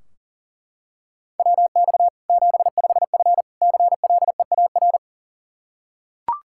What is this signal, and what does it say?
Morse code, heard as a keyed tone, about 40 wpm, 700 Hz. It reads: WX 75F CLEAR
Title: Practice QSO - Exchange 2 - Info - 40wpm